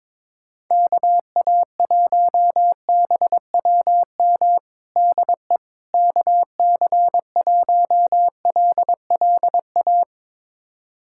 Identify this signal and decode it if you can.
Morse code, heard as a keyed tone, about 22 wpm, 700 Hz. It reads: KA1BWM DE KC1LLA
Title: USA Callsign DE Callsign 22wpm